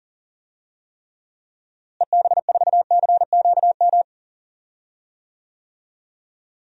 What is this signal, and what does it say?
Morse code, heard as a keyed tone, about 40 wpm, 700 Hz. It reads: EB4CQM
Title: DXCC-100 Call Signs - 6 Characters - Repeated 3x - 40wpm